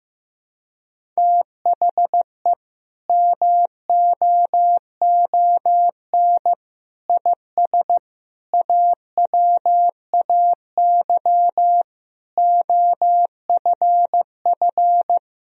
Morse code, heard as a keyed tone, about 15 wpm, 700 Hz. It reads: THE MOON IS AWAY OFF
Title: Sentences from Top 1000 Words - Review Part 3 - 15wpm